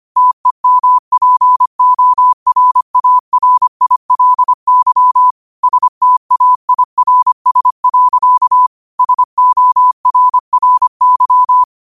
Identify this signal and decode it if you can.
Morse code, heard as a keyed tone, about 25 wpm, 1000 Hz. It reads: TEMPORARILY STAIRS. SORRY
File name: qotdfiles / 25wpm